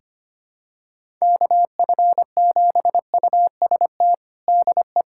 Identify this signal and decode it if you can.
Morse code, heard as a keyed tone, about 25 wpm, 700 Hz. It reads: KF7UST DE
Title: USA Callsign DE Callsign 25wpm